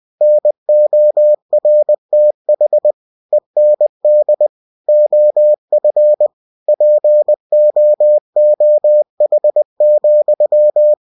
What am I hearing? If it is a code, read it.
Morse code, heard as a keyed tone, about 20 wpm, 600 Hz. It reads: NORTH END OF POOH,
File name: cw-books / winnie-the-pooh / winnie-20wpm-page-0019